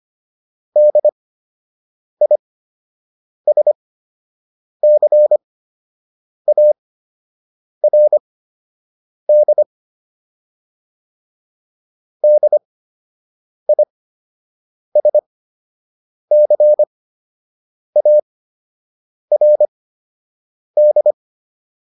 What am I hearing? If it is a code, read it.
Morse code, heard as a keyed tone, about 25 wpm, spaced as for 7 wpm, 600 Hz. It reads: DISCARD DISCARD
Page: fundamental-practice-files